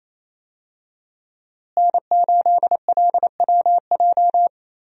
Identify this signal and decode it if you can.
Morse code, heard as a keyed tone, about 28 wpm, 700 Hz. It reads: N8LWJ
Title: USA Call Signs - 5 Characters - Repeated 3x - 28wpm